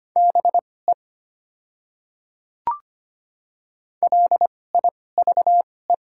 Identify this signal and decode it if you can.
Morse code, heard as a keyed tone, about 25 wpm, spaced as for 18 wpm, 690 Hz. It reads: BE LIVE
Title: 2x Character Spacing - Top 200 Words - 25wpm